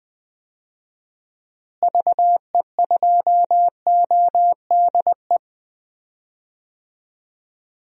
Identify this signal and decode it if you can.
Morse code, heard as a keyed tone, about 20 wpm, 700 Hz. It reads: VE2ODE
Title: DXCC-100 Call Signs - 6 Characters - Repeated 2x - 20wpm